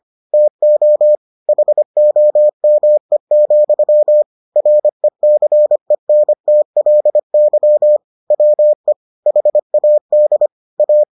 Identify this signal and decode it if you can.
Morse code, heard as a keyed tone, about 25 wpm, 600 Hz. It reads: TO HOME, RECENTLY WE HAD A